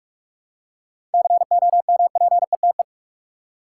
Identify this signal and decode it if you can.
Morse code, heard as a keyed tone, about 44 wpm, 700 Hz. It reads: COMPETE
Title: Seven Letter Words - 45wpm